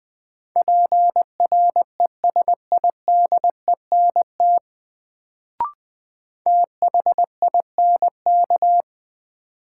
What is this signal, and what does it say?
Morse code, heard as a keyed tone, about 20 wpm, 700 Hz. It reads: PRESIDENT THINK
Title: Sets of 2 Words - New Word Drill - 20wpm